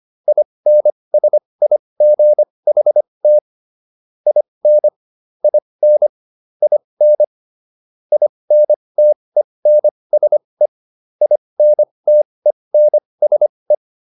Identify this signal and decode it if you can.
Morse code, heard as a keyed tone, about 25 wpm, spaced as for 20 wpm, 600 Hz. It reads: INSIGHT IN IN IN INTENSE INTENSE